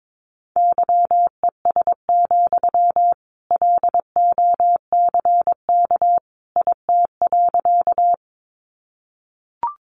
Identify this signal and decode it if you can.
Morse code, heard as a keyed tone, about 22 wpm, 700 Hz. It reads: YES, LOCK IT.
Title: Sentences from Top 1300 Words 22wpm